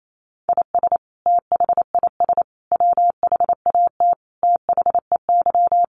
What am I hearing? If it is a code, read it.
Morse code, heard as a keyed tone, about 28 wpm, 700 Hz. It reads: IS THIS WHAT THEY
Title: Sentences from Top 1500 Words 28wpm